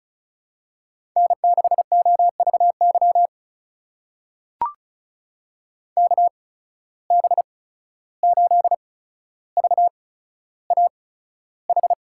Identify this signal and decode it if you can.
Morse code, heard as a keyed tone, about 36 wpm, spaced as for 14 wpm, 700 Hz. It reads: N6OVY KB8VAH
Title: USA Call Signs - 8x Character Spacing - 35wpm